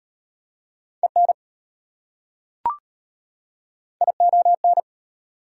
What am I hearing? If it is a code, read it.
Morse code, heard as a keyed tone, about 36 wpm, 700 Hz. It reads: EN ION